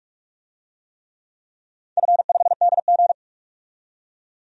Morse code, heard as a keyed tone, about 46 wpm, 700 Hz. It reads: F5DG
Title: DXCC-100 Call Signs - 4 Characters - 45wpm